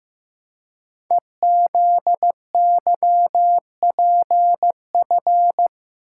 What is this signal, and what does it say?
Morse code, heard as a keyed tone, about 15 wpm, 700 Hz. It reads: EZYPF
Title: Code Groups - 5 Letters - 15wpm